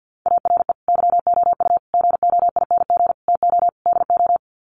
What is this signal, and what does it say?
Morse code, heard as a keyed tone, about 51 wpm, 700 Hz. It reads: ARE YOU GOING TO DO
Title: Sets of 5 Words 50wpm